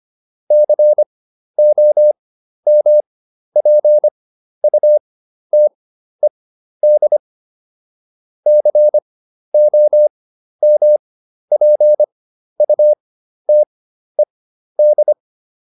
Morse code, heard as a keyed tone, about 25 wpm, spaced as for 12 wpm, 600 Hz. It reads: COMPUTED COMPUTED